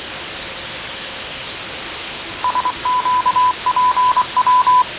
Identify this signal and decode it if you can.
Morse code, heard as a keyed tone, about 24 wpm, 1000 Hz, with radio noise. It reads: SQPW
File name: fav22-6824usb-20251029-1420z-voice-test-win10